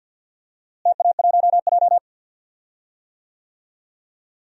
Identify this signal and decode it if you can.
Morse code, heard as a keyed tone, about 48 wpm, 700 Hz. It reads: TA1J
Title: DXCC-100 Call Signs - 4 Characters - Repeated 3x - 50wpm